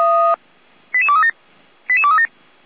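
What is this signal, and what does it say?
Five-tone selective call (ZVEI), heard as quick runs of five tones, tones of about 70 ms. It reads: 88237 88237
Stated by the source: ZVEI five-tone selective call received by SDR.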